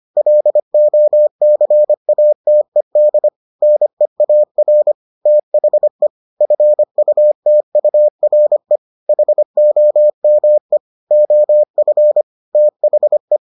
Morse code, heard as a keyed tone, about 25 wpm, 600 Hz. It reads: LOCATED NEAR THE FUTURE HOME OF THE